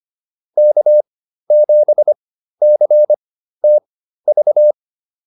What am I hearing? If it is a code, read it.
Morse code, heard as a keyed tone, about 25 wpm, spaced as for 13 wpm, 600 Hz. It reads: K7CTV
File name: QSO204 13